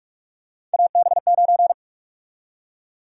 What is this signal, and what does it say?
Morse code, heard as a keyed tone, about 46 wpm, 700 Hz. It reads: AB9
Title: USA Call Sign Prefixes - Repeated 2x - 45wpm